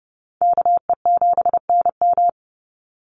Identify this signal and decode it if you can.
Morse code, heard as a keyed tone, about 30 wpm, 700 Hz. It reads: KE7NM
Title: USA Call Signs 30wpm